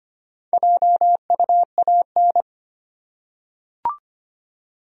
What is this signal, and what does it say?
Morse code, heard as a keyed tone, about 25 wpm, 700 Hz. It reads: JUAN
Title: USA Common Names - 25wpm